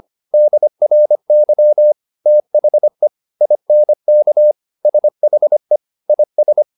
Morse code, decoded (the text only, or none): DRY THE INK SHE IS